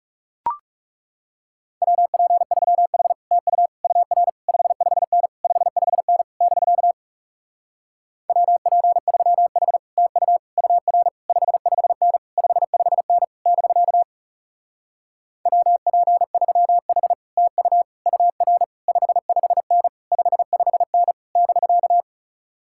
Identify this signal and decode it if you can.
Morse code, heard as a keyed tone, about 40 wpm, 700 Hz. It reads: WP3H TU UR 55N 55N <BK> WP3H TU UR 55N 55N <BK> WP3H TU UR 55N 55N <BK>